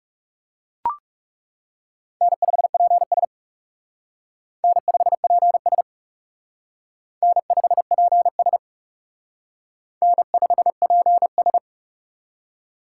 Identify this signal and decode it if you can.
Morse code, heard as a keyed tone, about 37 wpm, 700 Hz. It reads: N5PS N5PS N5PS N5PS
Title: Speed Racing - USA 1x2 Call Signs 30wpm